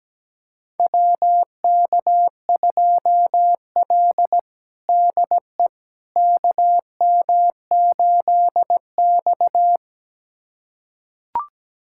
Morse code, heard as a keyed tone, about 17 wpm, 700 Hz. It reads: WK2L DE KM8X
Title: USA Callsign DE Callsign - 4 Characters - 17wpm